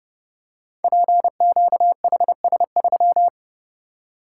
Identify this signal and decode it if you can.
Morse code, heard as a keyed tone, about 30 wpm, 700 Hz. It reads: PQHS3
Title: Code Groups - 5 Letter:Numbers - 30wpm